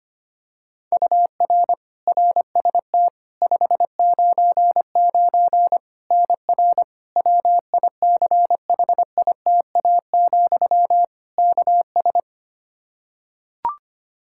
Morse code, heard as a keyed tone, about 25 wpm, 700 Hz. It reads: UR RST 599 NR WICHITA, KS
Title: K1USN SST QSO Practice 25wpm